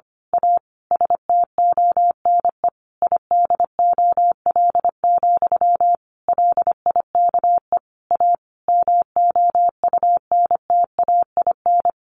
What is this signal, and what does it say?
Morse code, heard as a keyed tone, about 25 wpm, 700 Hz. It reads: A STONE IDOL, LIKE A MOUNTAIN
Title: A Sound Of Thunder 25wpm